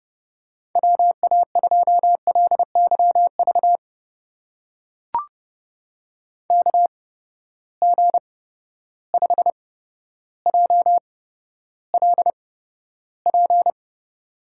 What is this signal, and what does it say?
Morse code, heard as a keyed tone, about 31 wpm, spaced as for 13 wpm, 700 Hz. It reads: WA2LYV KG5JLP
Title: USA Call Signs - 6 Characters - 8x Character Spacing - 30wpm